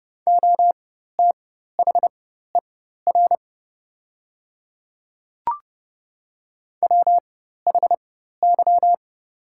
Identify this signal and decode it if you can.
Morse code, heard as a keyed tone, about 30 wpm, spaced as for 14 wpm, 700 Hz. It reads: OTHER WHY
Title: 4x Character Spacing - Top 200 Words - 30wpm